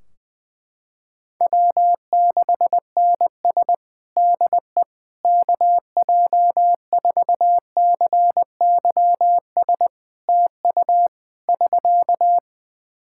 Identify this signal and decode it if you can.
Morse code, heard as a keyed tone, about 20 wpm, 700 Hz. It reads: W6NS DE KJ4CYS TU <SK>
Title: Practice QSO - Exchange 4 - Ending - 20wpm